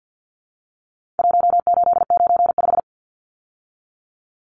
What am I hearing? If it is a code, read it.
Morse code, heard as a keyed tone, about 50 wpm, 700 Hz. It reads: 1895
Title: Numbers - Contest Style - 4-Digit 50wpm